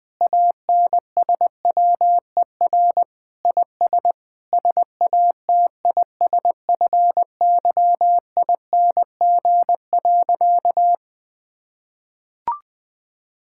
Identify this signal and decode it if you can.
Morse code, heard as a keyed tone, about 21 wpm, 700 Hz. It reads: ANSWER IS SATISFYING.
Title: Sentences with 3-Word Phrases - Spoken Beginning - 20wpm